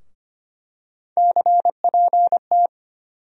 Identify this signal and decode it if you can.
Morse code, heard as a keyed tone, about 25 wpm, 700 Hz. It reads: CPT
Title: USA Call Sign Postfixes - 3 Letters 25wpm